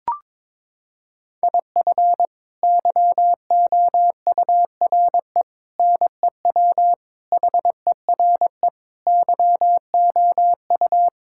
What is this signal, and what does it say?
Morse code, heard as a keyed tone, about 22 wpm, 700 Hz. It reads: IF YOURE NEW HERE YOU
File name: practice / 22wpm / Sets of 8 Words - 22wpm